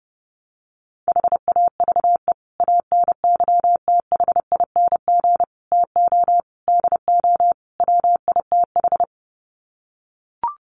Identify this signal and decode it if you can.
Morse code, heard as a keyed tone, about 30 wpm, 700 Hz. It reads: HAVE ANYTHING TO DO WITH
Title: Sets of 5 Words 30wpm